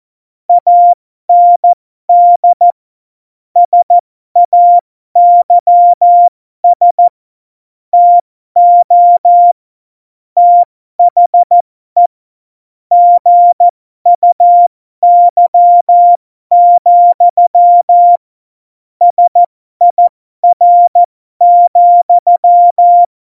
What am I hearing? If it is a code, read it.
Morse code, heard as a keyed tone, about 14 wpm, spaced as for 12 wpm, 700 Hz. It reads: AND SAYS TO THE GUY, SIR,
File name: Morsekurs / Witze